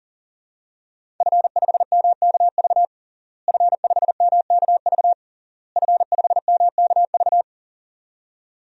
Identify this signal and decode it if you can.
Morse code, heard as a keyed tone, about 37 wpm, 700 Hz. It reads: F5MKV F5MKV F5MKV
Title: DXCC-100 Call Signs - 5 Characters - Repeated 3x - 40wpm